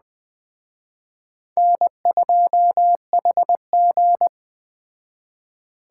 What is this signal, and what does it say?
Morse code, heard as a keyed tone, about 19 wpm, 700 Hz. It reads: N2HG